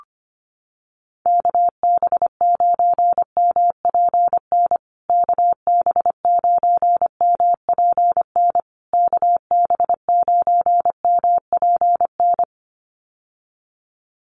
Morse code, heard as a keyed tone, about 25 wpm, 700 Hz. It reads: KB9MPN KB9MPN KB9MPN
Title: USA Call Signs Repeated 3x 25wpm